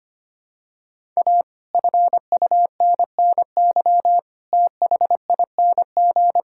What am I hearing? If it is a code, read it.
Morse code, heard as a keyed tone, about 25 wpm, 700 Hz. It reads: A FUNNY THING
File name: practice / 25wpm / Sentences with 9-Word Phrases - 25wpm